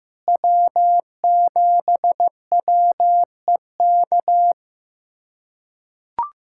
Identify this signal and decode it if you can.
Morse code, heard as a keyed tone, about 15 wpm, 700 Hz. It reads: W7WEK
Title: USA Callsign DE Callsign - 5 Characters - 15wpm